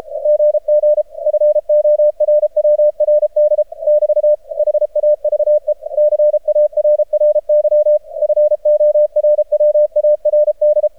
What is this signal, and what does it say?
Morse code, heard as a keyed tone, about 30 wpm, 600 Hz. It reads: GG FORWRD <BT> HAVE CARRY FORWARD
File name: archived / CRSnet